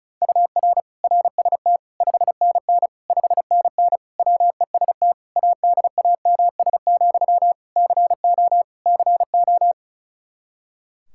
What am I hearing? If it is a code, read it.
Morse code, heard as a keyed tone, about 34 wpm, 700 Hz. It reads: UR RST 5NN 5NN WEST ADAMS, CO CO
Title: Practice QSO - Exchange 1 - Beginning - 35wpm